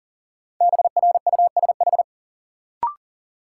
Morse code, heard as a keyed tone, about 40 wpm, 700 Hz. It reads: BRUSH
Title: Sentences from Top 1100 Words - New Word Drill - 40wpm